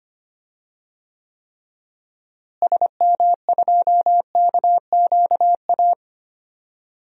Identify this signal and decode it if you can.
Morse code, heard as a keyed tone, about 25 wpm, 700 Hz. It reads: SM2KQA
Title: DXCC-100 Call Signs - 6 Characters - Repeated 3x - 25wpm